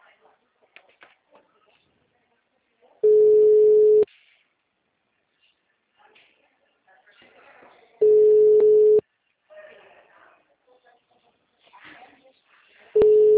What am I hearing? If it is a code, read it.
Ringback tone (Europe, 425 Hz), with speech.